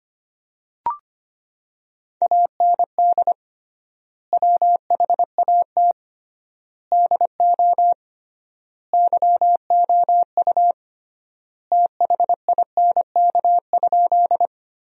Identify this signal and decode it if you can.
Morse code, heard as a keyed tone, about 25 wpm, 700 Hz. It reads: AND WHAT DO YOU THINK?
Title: 3x Word-Spacing - Sentences from Top 200 Words - 25wpm